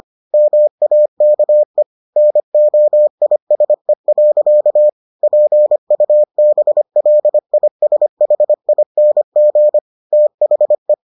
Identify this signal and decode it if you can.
Morse code, heard as a keyed tone, about 25 wpm, 600 Hz. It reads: MAKE NOISE. PUBLISHING THE